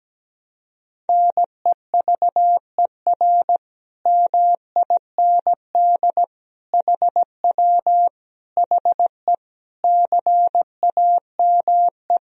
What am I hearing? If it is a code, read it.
Morse code, heard as a keyed tone, about 17 wpm, 700 Hz. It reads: NEVER MIND HW HE CAME
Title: Sentences from Top 500 Words and Abbreviations 17wpm